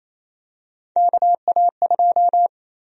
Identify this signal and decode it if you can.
Morse code, heard as a keyed tone, about 28 wpm, 700 Hz. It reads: KA2
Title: USA Call Sign Prefixes - 28wpm